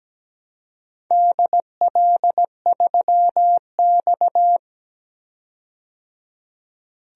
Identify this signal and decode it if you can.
Morse code, heard as a keyed tone, about 17 wpm, 700 Hz. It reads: DL3X
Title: DXCC-100 Call Signs - 4 Characters - 17wpm